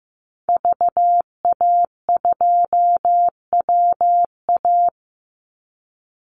Morse code, heard as a keyed tone, about 15 wpm, 700 Hz. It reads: VA2WA
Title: Call Signs Repeated 3x - 15wpm